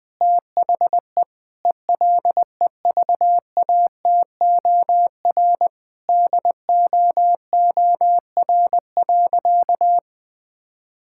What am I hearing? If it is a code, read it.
Morse code, heard as a keyed tone, about 21 wpm, 700 Hz. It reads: THE ELEVATOR DOOR.
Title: Sentences from Top 1600 Words 20wpm